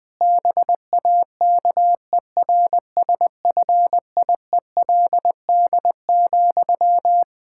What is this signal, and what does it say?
Morse code, heard as a keyed tone, about 21 wpm, 700 Hz. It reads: BAKERSFIELD,
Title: Practice QSO - Exchange 1 - Beginning - 20wpm